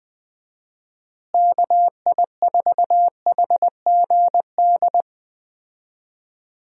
Morse code, heard as a keyed tone, about 21 wpm, 700 Hz. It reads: KI4HGD
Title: USA Call Signs - 6 Characters - Repeated 2x - 20wpm